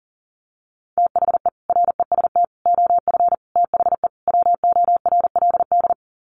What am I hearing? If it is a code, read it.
Morse code, heard as a keyed tone, about 40 wpm, 700 Hz. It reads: THE REST OF THE WORLD
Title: Sets of 5 Words 40wpm